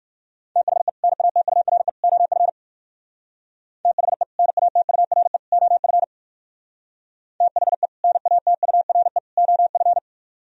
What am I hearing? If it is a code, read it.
Morse code, heard as a keyed tone, about 57 wpm, 700 Hz. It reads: THE NATURE OF THE NATURE OF THE NATURE OF